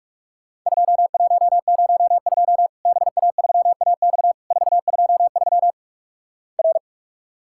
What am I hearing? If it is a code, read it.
Morse code, heard as a keyed tone, about 48 wpm, 700 Hz. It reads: 2102 BA3AX 423 R
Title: Worked All Europe DX Contest - QTC Exchange - 45wpm